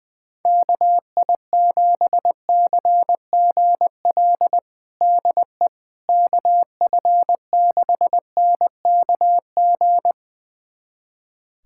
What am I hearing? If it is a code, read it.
Morse code, heard as a keyed tone, about 20 wpm, 700 Hz. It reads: KI7CGL DE KF6NKG
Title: USA Callsign DE Callsign - 6 Characters - 20wpm